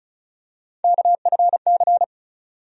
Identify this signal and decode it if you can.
Morse code, heard as a keyed tone, about 36 wpm, 700 Hz. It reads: KFC
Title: USA Call Sign Postfixes - 3 Letters 35wpm